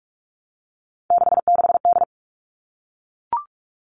Morse code, heard as a keyed tone, about 44 wpm, 700 Hz. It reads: 66D
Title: All Cut Numbers - 3-Digit 45wpm